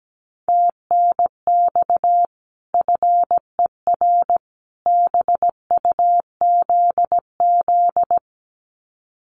Morse code, heard as a keyed tone, about 17 wpm, 700 Hz. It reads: TNX FER BUZZ
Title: Common QSO Phrases 17wpm